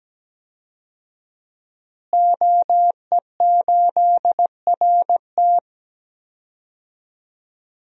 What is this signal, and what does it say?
Morse code, heard as a keyed tone, about 17 wpm, 700 Hz. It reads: OE8RT
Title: DXCC-100 Call Signs - 5 Characters - Repeated 3x - 17wpm